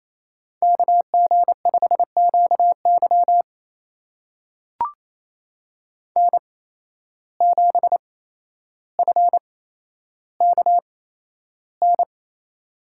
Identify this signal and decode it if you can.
Morse code, heard as a keyed tone, about 28 wpm, spaced as for 12 wpm, 700 Hz. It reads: KG5QY N7FKN
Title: USA Call Signs - 5 Characters - 8x Character Spacing - 28wpm